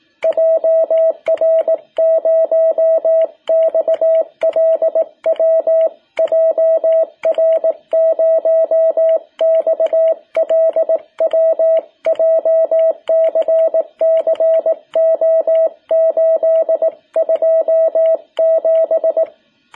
Morse code, heard as a keyed tone, about 18 wpm, 660 Hz. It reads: JR0XLW JR0XLW JCC O827
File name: jh8cbh / 2021cw